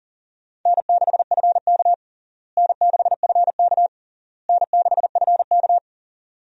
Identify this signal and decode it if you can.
Morse code, heard as a keyed tone, about 40 wpm, 700 Hz. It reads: N6FK N6FK N6FK
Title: USA Call Signs - 4 Characters - Repeated 3x - 40wpm